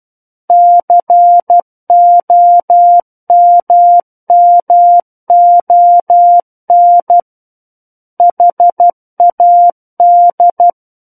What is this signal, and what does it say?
Morse code, heard as a keyed tone, about 12 wpm, 700 Hz. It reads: COMMON HAD
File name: Book One / 03-Chapter 3 - 12 WPM